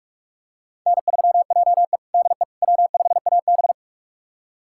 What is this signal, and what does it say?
Morse code, heard as a keyed tone, about 46 wpm, 700 Hz. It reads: N3JE DE W5AB
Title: USA Callsign DE Callsign - 4 Characters - 45wpm